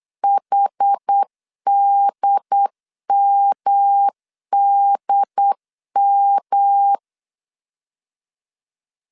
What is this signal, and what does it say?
Morse code, heard as a keyed tone, about 8 wpm, 800 Hz. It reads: HDMDM
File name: CW / DET / 1 3 1500C S B 8wpm 4141